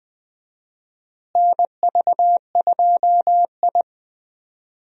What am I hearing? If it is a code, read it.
Morse code, heard as a keyed tone, about 20 wpm, 700 Hz. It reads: NV2I
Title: USA Call Signs - 4 Characters - Repeated 2x - 20wpm